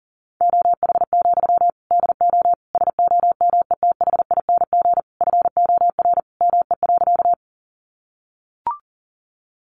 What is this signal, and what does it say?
Morse code, heard as a keyed tone, about 40 wpm, 700 Hz. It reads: OH, DO SOMETHING FOR ME.
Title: Sentences from Top 600 Words 40wpm